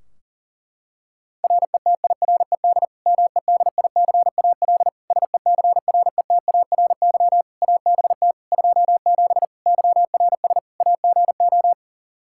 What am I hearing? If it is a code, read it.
Morse code, heard as a keyed tone, about 40 wpm, 700 Hz. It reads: RETIRED MEDICAL SECRETARY ABT 27 YRS AGO
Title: Practice QSO - Exchange 3 - More Info - 40wpm